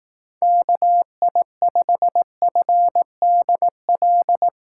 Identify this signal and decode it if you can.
Morse code, heard as a keyed tone, about 18 wpm, 700 Hz. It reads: KI5FDL
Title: Speed Racing - USA 2x3 Call Signs 15wpm